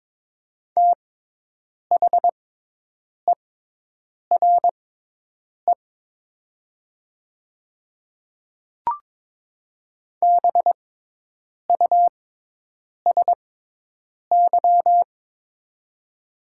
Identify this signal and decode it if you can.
Morse code, heard as a keyed tone, about 22 wpm, spaced as for 8 wpm, 700 Hz. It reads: THERE BUSY
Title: 6x Character Spacing - Top 500 Words - 22wpm